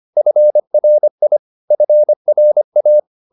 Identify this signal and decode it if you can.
Morse code, heard as a keyed tone, about 25 wpm, 600 Hz. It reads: FRI FRA
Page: Ferie 25 wpm